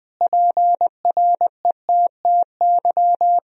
Morse code, heard as a keyed tone, about 21 wpm, 700 Hz. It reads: PRETTY
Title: Sentences from Top 1000 Words - Review Part 3 - 20wpm